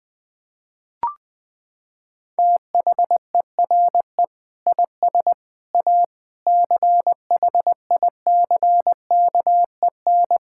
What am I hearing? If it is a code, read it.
Morse code, heard as a keyed tone, about 20 wpm, 700 Hz. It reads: THERE IS A CHICKEN
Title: Sentences from Top 1000 Words - Review Part 5 - 20wpm